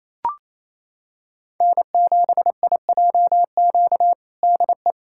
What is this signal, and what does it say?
Morse code, heard as a keyed tone, about 28 wpm, 700 Hz. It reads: N7IJQ DE
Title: USA Callsign DE Callsign 28wpm